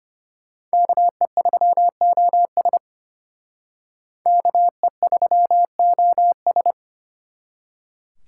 Morse code, heard as a keyed tone, about 32 wpm, 700 Hz. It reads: KE3OS KE3OS